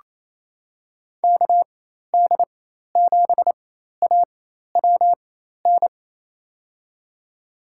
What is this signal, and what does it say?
Morse code, heard as a keyed tone, about 28 wpm, spaced as for 13 wpm, 700 Hz. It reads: KD7AWN
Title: USA Call Signs - 6 Characters - 4x Character Spacing - 28wpm